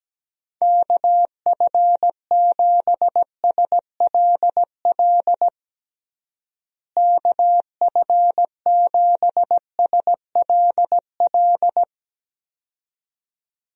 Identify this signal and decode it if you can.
Morse code, heard as a keyed tone, about 17 wpm, 700 Hz. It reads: KF7SLL KF7SLL
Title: USA Call Signs - 6 Characters - Repeated 2x - 17wpm